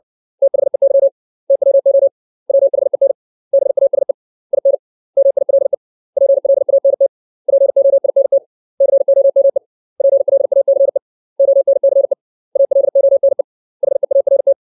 Morse code, heard as a keyed tone, about 60 wpm, 550 Hz. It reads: THEY TWO WHEN BASE IN MILE PLANT POINT COME PLACE ONCE ALONE HEART
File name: 632 words / 632 Words 60WPM